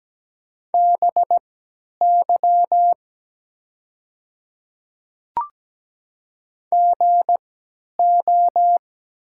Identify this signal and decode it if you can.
Morse code, heard as a keyed tone, about 17 wpm, spaced as for 10 wpm, 700 Hz. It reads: BY GO
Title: 3x Character Spacing - Top 200 Words - 17wpm